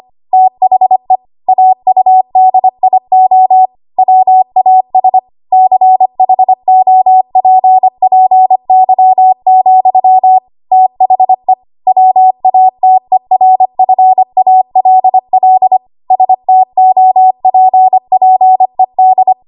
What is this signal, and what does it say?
Morse code, heard as a keyed tone, about 25 wpm, 750 Hz. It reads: THE AUDIO WAS CHOPPY, THE WATERFALL STOPPED
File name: Archive / 25 WPM / 25WPM